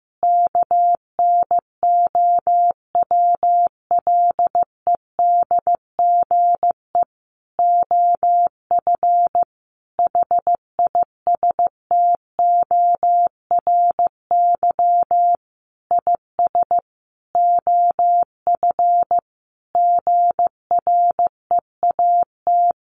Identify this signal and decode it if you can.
Morse code, heard as a keyed tone, about 14 wpm, 700 Hz. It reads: KNOWLEDGE OF HISTORY IS OF GREAT
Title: Sentences from Top 900 Words 15wpm